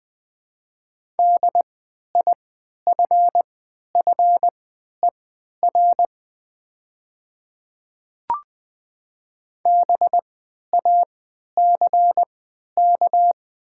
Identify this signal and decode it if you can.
Morse code, heard as a keyed tone, about 21 wpm, spaced as for 12 wpm, 700 Hz. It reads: DIFFER BACK
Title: 3x Character Spacing - Top 200 Words - 20wpm